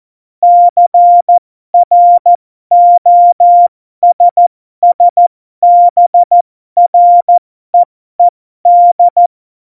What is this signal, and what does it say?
Morse code, heard as a keyed tone, about 14 wpm, spaced as for 12 wpm, 700 Hz. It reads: CROSSBREED
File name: Morsekurs / Witze